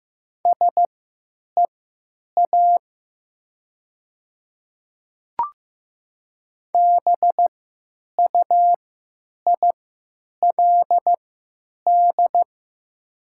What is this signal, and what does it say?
Morse code, heard as a keyed tone, about 15 wpm, spaced as for 9 wpm, 700 Hz. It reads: SEA BUILD
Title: 3x Character Spacing - Top 300 Words - 15wpm